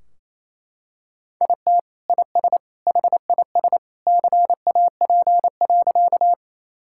Morse code, heard as a keyed tone, about 28 wpm, 700 Hz. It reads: IT IS HIS CAP.
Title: Sentences from Top 1600 Words 28wpm